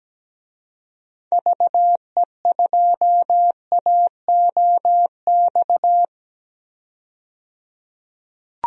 Morse code, heard as a keyed tone, about 17 wpm, 700 Hz. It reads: VE2AOX
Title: DXCC-100 Call Signs - 6 Characters - 17wpm